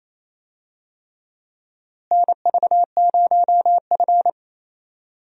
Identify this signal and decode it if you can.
Morse code, heard as a keyed tone, about 27 wpm, 700 Hz. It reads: NV0F
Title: USA Call Signs - 4 Characters - Repeated 3x - 28wpm